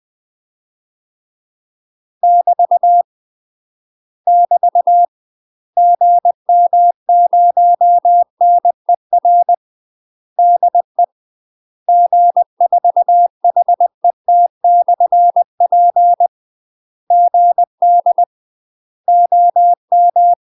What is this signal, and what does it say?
Morse code, heard as a keyed tone, about 20 wpm, 700 Hz. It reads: <BT> <BT> GM0NER DE G4HET/P GD OM
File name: UK MASTER QSOs TRAINING / 20WPM / UK MASTER QSOs 0062